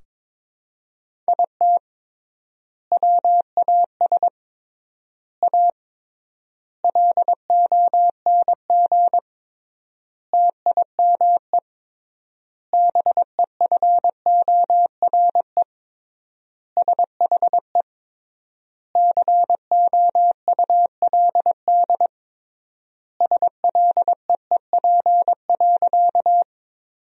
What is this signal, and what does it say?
Morse code, heard as a keyed tone, about 22 wpm, 700 Hz. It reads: IT WAS A LONG TIME BEFORE SHE COULD SLEEP.
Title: 3x Word-Spacing - Sentences from Top 500 Words - 22wpm